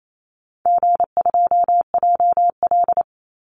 Morse code, heard as a keyed tone, about 28 wpm, 700 Hz. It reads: G2JL
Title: Easy Call Signs - 28wpm